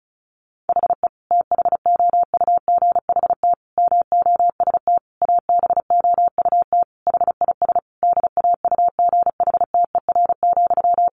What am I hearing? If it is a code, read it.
Morse code, heard as a keyed tone, about 34 wpm, 700 Hz. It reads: HE THOUGHT MOST ABOUT HIS DAUGHTER,
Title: Sentences from Top 1500 Words 35wpm